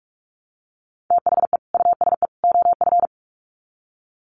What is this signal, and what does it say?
Morse code, heard as a keyed tone, about 44 wpm, 700 Hz. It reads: THE USE OF
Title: Sets of 3 Words 45wpm